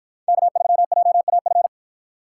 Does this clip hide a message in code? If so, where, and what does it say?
Yes: Morse code from 0.3 to 1.7 s, heard as a keyed tone, about 53 wpm, 700 Hz. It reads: K3JAF